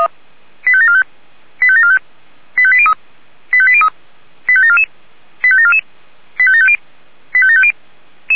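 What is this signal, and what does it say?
Five-tone selective call (ZVEI), heard as quick runs of five tones, tones of about 70 ms. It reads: 85745 85745 85803 85803 85744 85744 85750 85750
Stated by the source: ZVEI five-tone selective call received on an SDR.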